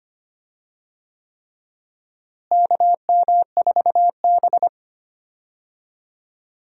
Morse code, heard as a keyed tone, about 25 wpm, 700 Hz. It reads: KM4B
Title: USA Call Signs - 4 Characters - Repeated 3x - 25wpm